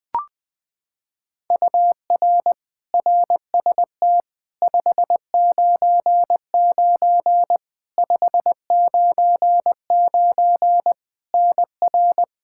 Morse code, heard as a keyed tone, about 20 wpm, 700 Hz. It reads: UR RST 599 599 NR
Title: Practice QSO - Exchange 1 - Beginning - 20wpm